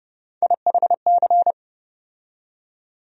Morse code, heard as a keyed tone, about 30 wpm, 700 Hz. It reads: IHC
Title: USA Call Sign Postfixes - 3 Letters 30wpm